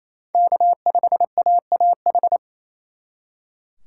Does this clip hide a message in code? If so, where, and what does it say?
Yes: Morse code from 0.3 to 2.4 s, heard as a keyed tone, about 35 wpm, 700 Hz. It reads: K5AAH